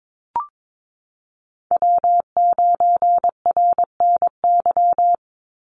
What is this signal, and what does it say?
Morse code, heard as a keyed tone, about 22 wpm, 700 Hz. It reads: W9RNY
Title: Call Signs - 22wpm